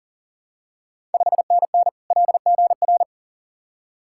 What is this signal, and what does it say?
Morse code, heard as a keyed tone, about 40 wpm, 690 Hz. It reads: 5NN LGR
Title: Ontario QSO Party Exchange 40wpm